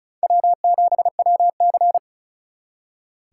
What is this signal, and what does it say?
Morse code, heard as a keyed tone, about 44 wpm, 700 Hz. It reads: W7WC